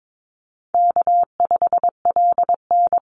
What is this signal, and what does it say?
Morse code, heard as a keyed tone, about 22 wpm, 700 Hz. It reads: K5LN
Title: Easy Call Signs Repeated 3x - 22wpm